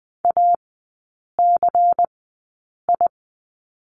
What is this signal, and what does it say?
Morse code, heard as a keyed tone, about 20 wpm, spaced as for 8 wpm, 700 Hz. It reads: ACI
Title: 100 Lesson 17e - Letter u - Three Characters 20wpm